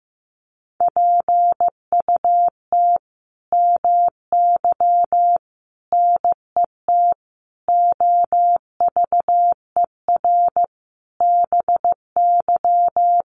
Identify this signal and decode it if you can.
Morse code, heard as a keyed tone, about 15 wpm, 700 Hz. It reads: PUT MY NET OVER BY
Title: Sentences from Top 1400 Words 15wpm